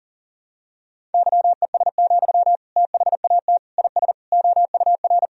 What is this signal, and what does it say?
Morse code, heard as a keyed tone, about 40 wpm, 700 Hz. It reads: YES, THAT IS OUR
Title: Sentences from Top 1000 Words - Review Part 3 - 40wpm